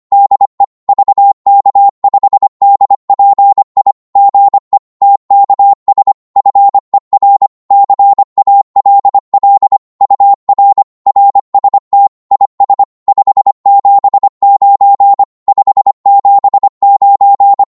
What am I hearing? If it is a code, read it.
Morse code, heard as a keyed tone, about 25 wpm, 800 Hz. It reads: DE VK5DPI GE TKS FER CALL UR RST IS 579 579
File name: Morse / qso-25wpm